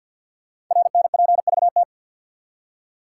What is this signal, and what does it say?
Morse code, heard as a keyed tone, about 50 wpm, 700 Hz. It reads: RNPVT
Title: Code Groups - 5 Letters - 50wpm